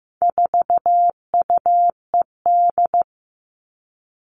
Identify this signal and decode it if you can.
Morse code, heard as a keyed tone, about 15 wpm, 700 Hz. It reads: 4UED
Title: All Cut Numbers - 4-Digit 15wpm